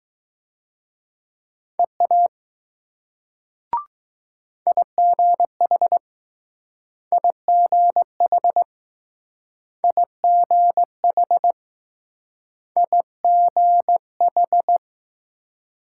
Morse code, heard as a keyed tone, about 20 wpm, 700 Hz. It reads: EA IGH IGH IGH IGH